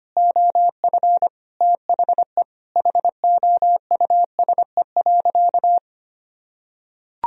Morse code, heard as a keyed tone, about 25 wpm, 700 Hz. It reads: OF THE HOUSE.
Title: Sentences from Top 1000 Words - Review Part 1 - 25wpm